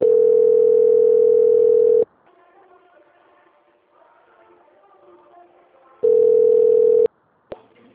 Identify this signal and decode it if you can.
Ringback tone (North America), 440 + 480 Hz.